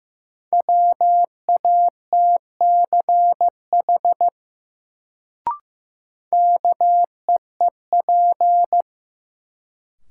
Morse code, heard as a keyed tone, about 15 wpm, 700 Hz. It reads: WATCH KEEP
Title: Sentences from Top 300 Words - New Word Drill - 15wpm